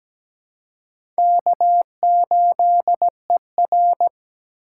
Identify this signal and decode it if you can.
Morse code, heard as a keyed tone, about 17 wpm, 700 Hz. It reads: K8ER
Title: USA Call Signs - 3x Character Spacing - 17wpm